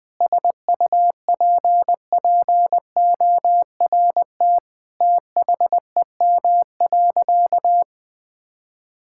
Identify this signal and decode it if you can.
Morse code, heard as a keyed tone, about 20 wpm, 700 Hz. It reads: SUPPORT THEM.
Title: Sentences from Top 1000 Words - Review Part 5 - 20wpm